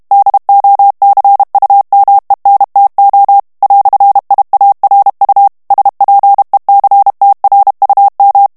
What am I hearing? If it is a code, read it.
Morse code, heard as a keyed tone, about 32 wpm, 800 Hz. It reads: DOCUMENTO "IARU SPECTRUM
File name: morse mp3 / ESPAGNOL TEXTE EN CLAIR 1 VITESSE 30 MOTS MINUTES